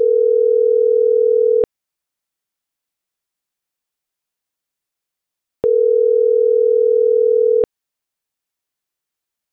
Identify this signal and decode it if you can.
Ringback tone (North America), 440 + 480 Hz.